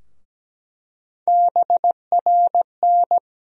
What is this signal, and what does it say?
Morse code, heard as a keyed tone, about 17 wpm, 700 Hz. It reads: BRN
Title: USA Call Sign Postfixes - 3 Letters 17wpm